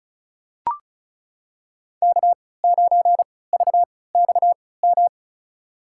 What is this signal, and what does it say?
Morse code, heard as a keyed tone, about 36 wpm, spaced as for 20 wpm, 700 Hz. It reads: K9VXM
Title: USA Call Signs - 5 Characters - 3x Character Spacing - 35wpm